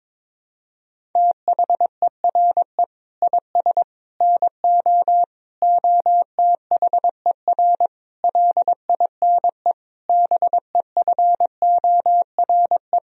Morse code, heard as a keyed tone, about 22 wpm, 700 Hz. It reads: THERE IS NO OTHER LINE BEFORE
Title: Sentences from Top 1000 Words - Review Part 1 - 22wpm